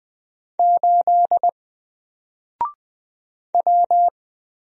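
Morse code, heard as a keyed tone, about 20 wpm, 700 Hz. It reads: 8 W
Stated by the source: Single letter and number Morse drill at 20 wpm, from N4LCD.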